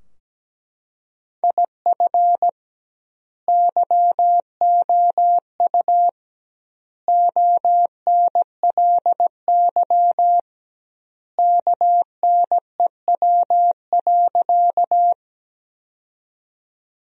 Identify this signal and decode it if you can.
Morse code, heard as a keyed tone, about 17 wpm, 700 Hz. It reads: IF YOU ONLY KNEW.
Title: 2x Word-Spacing - Sentences from Top 400 Words - 17wpm